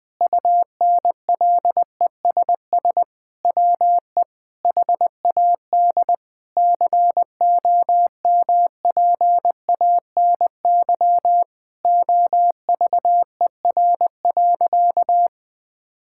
Morse code, heard as a keyed tone, about 20 wpm, 700 Hz. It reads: UNLESS WE HAD COMPANY OVER.
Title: Sentences from Top 1000 Words - Review Part 6 - 20wpm